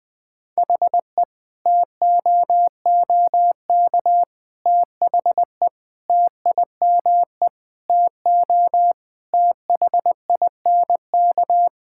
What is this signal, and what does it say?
Morse code, heard as a keyed tone, about 21 wpm, 700 Hz. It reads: HE TOOK THE TIME TO THINK
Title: Sentences from Top 1600 Words 20wpm